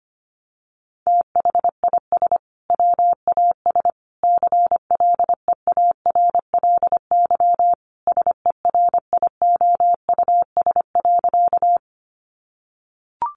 Morse code, heard as a keyed tone, about 25 wpm, 700 Hz. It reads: THIS WAS CLEARLY SERIOUS.
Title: Sentences from Top 1300 Words 25wpm